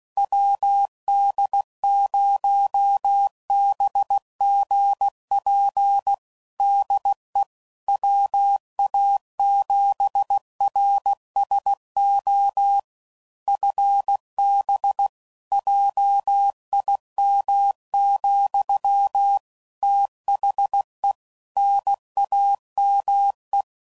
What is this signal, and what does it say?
Morse code, heard as a keyed tone, about 16 wpm, 800 Hz. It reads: WD0BGP DE WA7RSO FB JIM, THE NAME
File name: CW Code Groups @15WPM / QSOs / WD0BGP Spaced